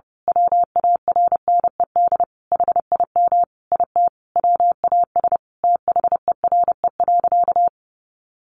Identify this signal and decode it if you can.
Morse code, heard as a keyed tone, about 30 wpm, 700 Hz. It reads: WARNED HIM IT WAS THERE.
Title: A Sound Of Thunder 30wpm